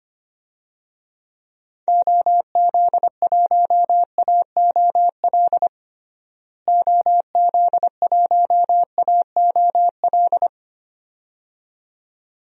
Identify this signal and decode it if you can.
Morse code, heard as a keyed tone, about 25 wpm, 700 Hz. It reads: OZ1AOL OZ1AOL
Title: DXCC-100 Call Signs - 6 Characters - Repeated 2x - 25wpm